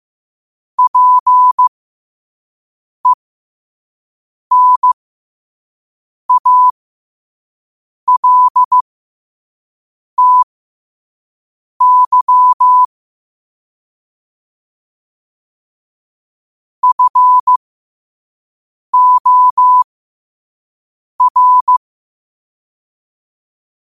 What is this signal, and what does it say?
Morse code, heard as a keyed tone, about 15 wpm, spaced as for 5 wpm, 1000 Hz. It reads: PENALTY FOR
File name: qotdfiles / 5wpm